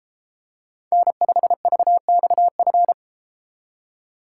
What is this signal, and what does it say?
Morse code, heard as a keyed tone, about 31 wpm, 700 Hz. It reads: N5VXF